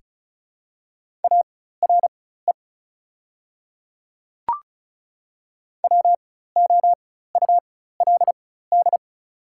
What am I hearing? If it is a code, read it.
Morse code, heard as a keyed tone, about 34 wpm, spaced as for 16 wpm, 700 Hz. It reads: ARE WOULD
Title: 4x Character Spacing - Top 100 Words - 35wpm